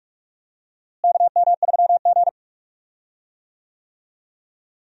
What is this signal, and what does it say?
Morse code, heard as a keyed tone, about 45 wpm, 700 Hz. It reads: KM3G
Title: USA Call Signs - 4 Characters - Repeated 2x - 45wpm